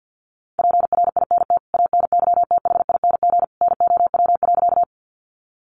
Morse code, heard as a keyed tone, about 48 wpm, 700 Hz. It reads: PRINT ANYTHING NOW.
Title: Sentences from Top 1400 Words 50wpm